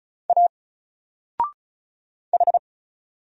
Morse code, heard as a keyed tone, about 36 wpm, 700 Hz. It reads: A H
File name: Single-Letter-Number-35wpm